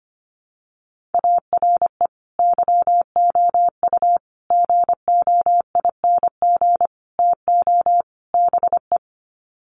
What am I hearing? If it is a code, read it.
Morse code, heard as a keyed tone, about 25 wpm, 700 Hz. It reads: ARE YOU GOING TO BE
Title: Sets of 5 Words Encore 25wpm